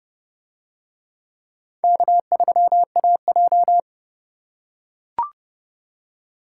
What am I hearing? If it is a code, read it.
Morse code, heard as a keyed tone, about 29 wpm, 700 Hz. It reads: K3AJ